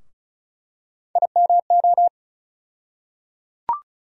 Morse code, heard as a keyed tone, about 36 wpm, 700 Hz. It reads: IMO
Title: USA Call Sign Postfixes - 3 Letters 35wpm